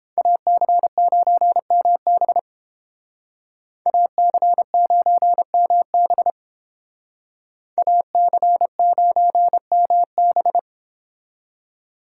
Morse code, heard as a keyed tone, about 30 wpm, 700 Hz. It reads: AC9MB AC9MB AC9MB